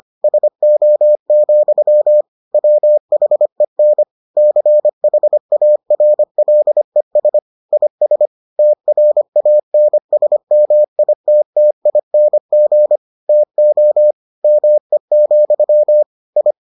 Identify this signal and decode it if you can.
Morse code, heard as a keyed tone, about 25 wpm, 600 Hz. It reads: SO, WHEN CHARLES IS TRANSMITTING TO ME, I